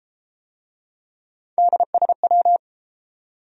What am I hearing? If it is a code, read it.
Morse code, heard as a keyed tone, about 30 wpm, 700 Hz. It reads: DSW